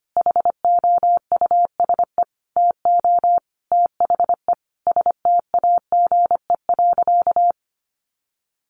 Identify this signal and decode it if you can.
Morse code, heard as a keyed tone, about 25 wpm, 700 Hz. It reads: HOUSE TO THE STAGE.
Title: Sentences from Top 1200 Words 25wpm